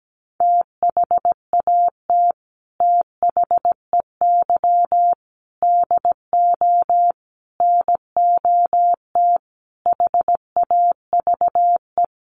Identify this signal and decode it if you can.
Morse code, heard as a keyed tone, about 17 wpm, 700 Hz. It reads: THAT THEY DO NOT HAVE
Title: Sets of 5 Words Encore2 17wpm